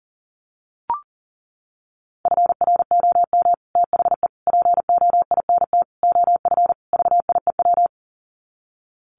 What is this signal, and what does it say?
Morse code, heard as a keyed tone, about 40 wpm, 700 Hz. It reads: FROM THE POINT OF VIEW
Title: Sets of 5 Words 40wpm